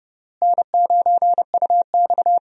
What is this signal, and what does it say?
Morse code, heard as a keyed tone, about 32 wpm, 700 Hz. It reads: N9UX
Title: USA Call Signs - 4 Characters - Repeated 3x - 30wpm